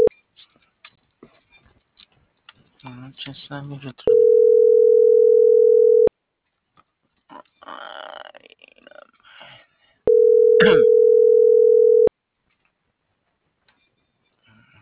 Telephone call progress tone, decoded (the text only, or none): ringback tone